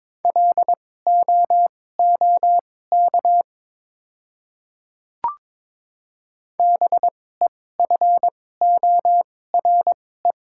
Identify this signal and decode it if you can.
Morse code, heard as a keyed tone, about 22 wpm, spaced as for 16 wpm, 700 Hz. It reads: LOOK BEFORE
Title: 2x Character Spacing - Top 300 Words - 22wpm